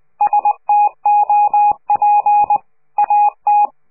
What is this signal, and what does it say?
Morse code, heard as a keyed tone, about 20 wpm, 830 Hz. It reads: STOP AT
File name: CQ de PCH laatste bericht farewell